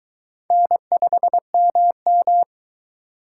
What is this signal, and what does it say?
Morse code, heard as a keyed tone, about 23 wpm, 700 Hz. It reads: N5MM